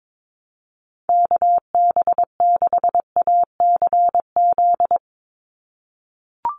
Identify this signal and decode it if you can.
Morse code, heard as a keyed tone, about 22 wpm, 700 Hz. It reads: KB6ACZ
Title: USA Call Signs 22wpm